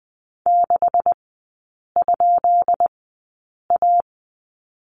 Morse code, heard as a keyed tone, about 20 wpm, spaced as for 8 wpm, 700 Hz. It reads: 6?A
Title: 172 Lesson 28e - Number 9 - Three Characters 20wpm